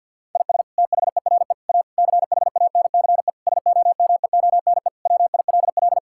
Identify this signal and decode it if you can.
Morse code, heard as a keyed tone, about 51 wpm, 700 Hz. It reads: IS THERE A CHANCE SOMEONE WILL
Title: Sentences from Top 1000 Words - Review Part 6 - 50wpm